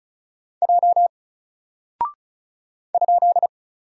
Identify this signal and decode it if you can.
Morse code, heard as a keyed tone, about 34 wpm, 700 Hz. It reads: J ?
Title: Single Character (Letter:Number:Punctuation:Prosign) 35wpm